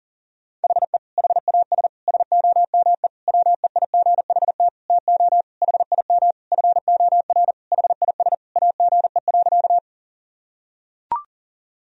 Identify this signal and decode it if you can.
Morse code, heard as a keyed tone, about 40 wpm, 700 Hz. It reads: HE HAS SOME WEIGHT TO HIM FOR HIS AGE.
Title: Sentences from Top 1000 Words - Review Part 3 - 40wpm